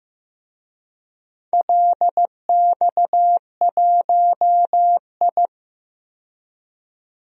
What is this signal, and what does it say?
Morse code, heard as a keyed tone, about 15 wpm, 700 Hz. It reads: LX1I
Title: DXCC-100 Call Signs - 4 Characters - 15wpm